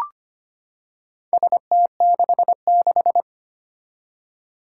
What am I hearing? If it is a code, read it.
Morse code, heard as a keyed tone, about 25 wpm, 700 Hz. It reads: ST66
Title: Four Challenging Letter-Number Combinations 25wpm